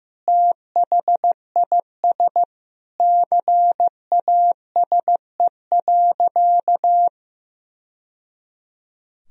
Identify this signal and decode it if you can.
Morse code, heard as a keyed tone, about 15 wpm, 700 Hz. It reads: THIS CASE.
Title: Sentences from Top 1000 Words - Review Part 4 - 15wpm